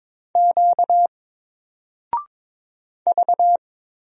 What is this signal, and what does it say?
Morse code, heard as a keyed tone, about 22 wpm, 700 Hz. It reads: Q V
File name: Single-Letter-Number-22wpm